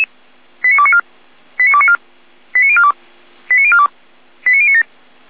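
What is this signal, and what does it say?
Five-tone selective call (ZVEI), heard as quick runs of five tones, tones of about 80 ms. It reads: 89294 89294 89042 89042 89097
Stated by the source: SDR-received ZVEI five-tone selective call.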